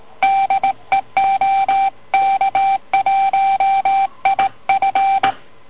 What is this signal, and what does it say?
Morse code, heard as a keyed tone, about 33 wpm, spaced as for 24 wpm, 780 Hz. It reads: DE OK1IF